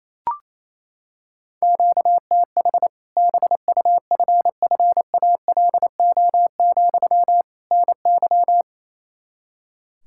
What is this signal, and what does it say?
Morse code, heard as a keyed tone, about 28 wpm, 700 Hz. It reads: QTH BUFFALO, NY
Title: Practice QSO - Exchange 1 - Beginning - 28wpm